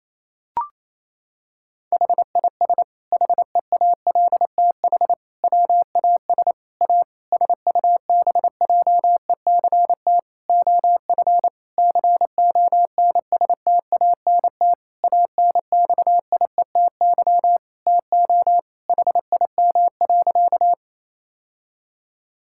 Morse code, heard as a keyed tone, about 28 wpm, 700 Hz. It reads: HIS HEALTH WAS A SUBJECT OF CONSTANT ANXIETY TO HIM.
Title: Sentences from Top 1600 Words 28wpm